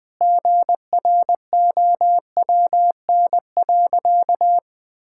Morse code, heard as a keyed tone, about 20 wpm, 700 Hz. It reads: GROWN.
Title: Sentences with 4-Word Phrases - 20wpm